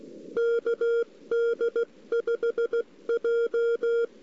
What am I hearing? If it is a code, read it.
Morse code, heard as a keyed tone, about 16 wpm, 470 Hz. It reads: KD5J
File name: CW / 2008-ss-cw-k5tr-op-n5rz -02:00